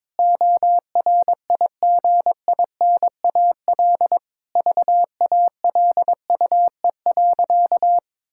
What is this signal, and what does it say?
Morse code, heard as a keyed tone, about 22 wpm, 700 Hz. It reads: ORIGINAL VALUE.
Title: Sentences from Top 1000 Words - Review Part 6 - 22wpm